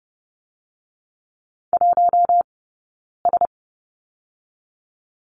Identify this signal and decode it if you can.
Morse code, heard as a keyed tone, about 30 wpm, 700 Hz. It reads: 1 S
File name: Lesson 19d - Letter w - Two Characters 30wpm